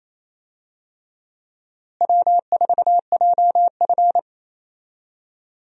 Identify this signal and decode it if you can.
Morse code, heard as a keyed tone, about 28 wpm, 700 Hz. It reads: W4JF
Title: USA Call Signs - 4 Characters - Repeated 3x - 28wpm